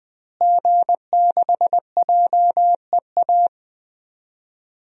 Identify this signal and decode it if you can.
Morse code, heard as a keyed tone, about 20 wpm, 700 Hz. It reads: G6JEA
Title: DXCC-100 Call Signs - 5 Characters - Repeated 3x - 20wpm